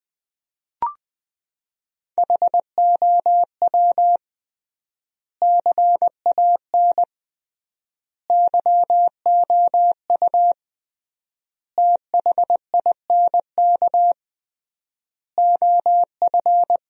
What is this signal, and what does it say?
Morse code, heard as a keyed tone, about 20 wpm, 700 Hz. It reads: HOW CAN YOU THINK OF
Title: 3x Word-Spacing - Sentences from Top 200 Words - 20wpm